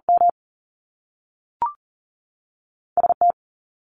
Morse code, heard as a keyed tone, about 40 wpm, 700 Hz. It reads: M ST
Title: Top Two Letter Combinations in Words 40wpm